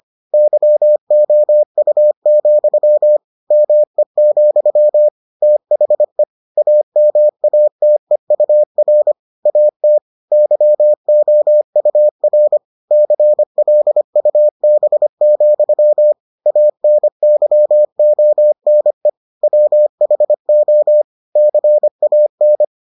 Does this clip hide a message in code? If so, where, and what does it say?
Yes: Morse code from 0.3 to 22.6 s, heard as a keyed tone, about 25 wpm, 600 Hz. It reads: YOU, ME, THE AMATEUR AT YOUR CLUB, ANYONE WHO CAN